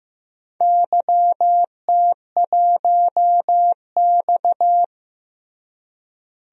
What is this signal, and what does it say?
Morse code, heard as a keyed tone, about 15 wpm, 700 Hz. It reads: YT1X
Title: DXCC-100 Call Signs - 4 Characters - Repeated 2x - 15wpm